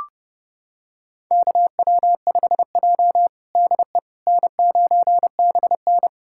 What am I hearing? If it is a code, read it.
Morse code, heard as a keyed tone, about 30 wpm, 700 Hz. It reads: KW5J DE N9BN
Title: USA Callsign DE Callsign - 4 Characters - 30wpm